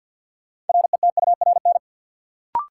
Morse code, heard as a keyed tone, about 47 wpm, 700 Hz. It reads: RETURN
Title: Sentences from Top 800 Words - New Word Drill - 50wpm